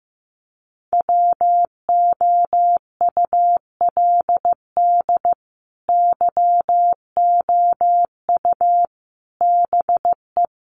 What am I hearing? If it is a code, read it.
Morse code, heard as a keyed tone, about 15 wpm, 700 Hz. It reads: WOULD YOU BE
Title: Sentences from Top 1400 Words 15wpm